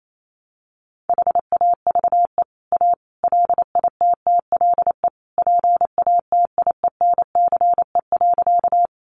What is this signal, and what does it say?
Morse code, heard as a keyed tone, about 28 wpm, 700 Hz. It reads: HAVE A LITTLE PATIENCE.
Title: Sentences from Top 1300 Words 28wpm